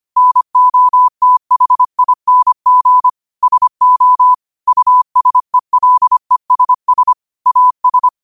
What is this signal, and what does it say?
Morse code, heard as a keyed tone, about 25 wpm, 1000 Hz. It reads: NOTHING SO USELESS AS
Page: New quotes every day in morse code at 25 Words per minute.